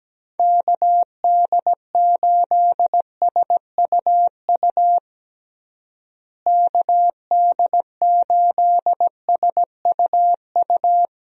Morse code, heard as a keyed tone, about 17 wpm, 700 Hz. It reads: KD8SUU KD8SUU
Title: USA Call Signs - 6 Characters - Repeated 2x - 17wpm